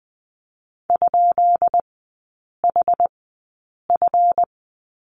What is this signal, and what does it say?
Morse code, heard as a keyed tone, about 20 wpm, spaced as for 8 wpm, 700 Hz. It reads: ?HF
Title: 218 Lesson 35e - Number 0 - Three Characters 20wpm